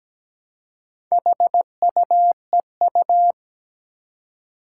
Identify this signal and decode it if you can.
Morse code, heard as a keyed tone, about 17 wpm, 700 Hz. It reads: HUEU
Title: Four Challenging Letter-Number Combinations 17wpm